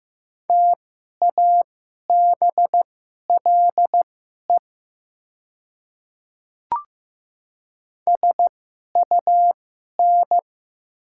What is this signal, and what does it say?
Morse code, heard as a keyed tone, about 15 wpm, spaced as for 11 wpm, 690 Hz. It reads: TABLE SUN
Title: 2x Character Spacing - Top 500 Words - 15wpm